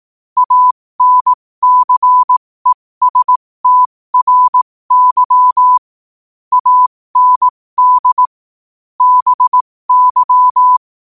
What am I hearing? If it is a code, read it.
Morse code, heard as a keyed tone, about 18 wpm, spaced as for 15 wpm, 1000 Hz. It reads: ANCESTRY AND BY
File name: qotdfiles / 15wpm